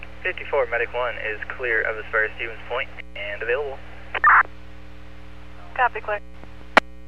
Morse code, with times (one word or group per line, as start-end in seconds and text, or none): none